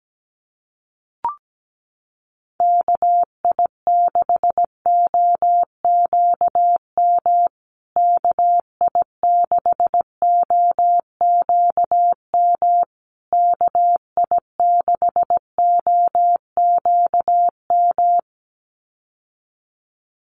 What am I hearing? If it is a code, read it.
Morse code, heard as a keyed tone, about 17 wpm, 700 Hz. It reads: KI6OQM KI6OQM KI6OQM
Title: USA Call Signs Repeated 3x 17wpm